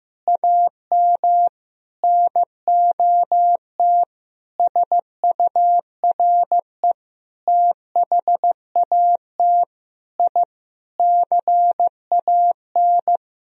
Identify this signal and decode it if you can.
Morse code, heard as a keyed tone, about 15 wpm, 700 Hz. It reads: AM NOT SURE THAT I CAN
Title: Sentences from Top 1000 Words - Review Part 3 - 15wpm